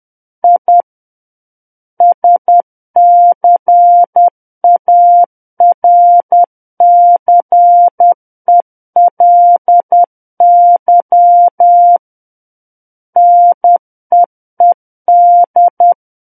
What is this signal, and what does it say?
Morse code, heard as a keyed tone, about 10 wpm, 700 Hz. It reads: I SCARCELY NEED
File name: Book One / 01-Chapter 1 - 10 WPM